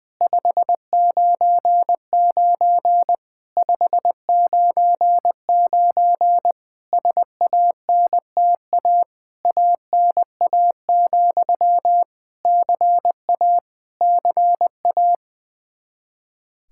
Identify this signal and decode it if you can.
Morse code, heard as a keyed tone, about 20 wpm, 700 Hz. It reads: 599 599 SANTA ANA, CA CA
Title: Practice QSO - Exchange 1 - Beginning - 20wpm